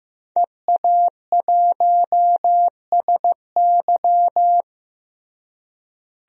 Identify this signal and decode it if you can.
Morse code, heard as a keyed tone, about 15 wpm, 700 Hz. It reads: EA1SY
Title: DXCC-100 Call Signs - 5 Characters - Repeated 3x - 15wpm